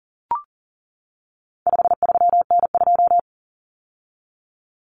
Morse code, heard as a keyed tone, about 40 wpm, 700 Hz. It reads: 53N2
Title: Common Cut Numbers - 4-Digit 40wpm